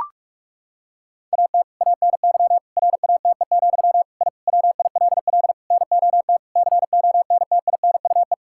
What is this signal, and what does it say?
Morse code, heard as a keyed tone, about 41 wpm, 700 Hz. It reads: AT ANY RATE, I WILL NOT CONTINUE
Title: Sentences from Top 1000 Words - Review Part 6 - 45wpm